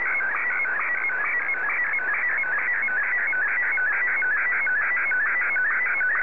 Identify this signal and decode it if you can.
SSTV (slow-scan television), Martin 1.
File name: sstv / M1-TO7